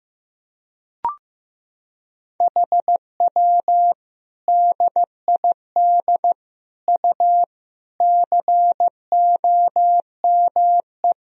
Morse code, heard as a keyed tone, about 16 wpm, 700 Hz. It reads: HW DID U COME
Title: Sentences from Top 100 Words and Abbreviations 15wpm